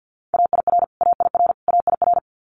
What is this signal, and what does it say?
Morse code, heard as a keyed tone, about 50 wpm, 700 Hz. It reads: AIR AIR AIR
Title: Top 500 Words Repeated 50wpm